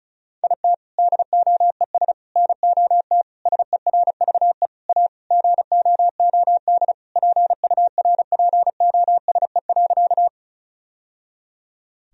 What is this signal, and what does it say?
Morse code, heard as a keyed tone, about 36 wpm, 700 Hz. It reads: IT DOES NOT SERVE A GOOD PURPOSE.
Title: Sentences from Top 1000 Words - Review Part 2 - 35wpm